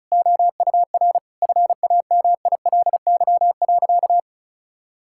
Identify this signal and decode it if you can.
Morse code, heard as a keyed tone, about 36 wpm, 700 Hz. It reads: OUR FAMILY.
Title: Sentences from Top 1000 Words - Review Part 6 - 35wpm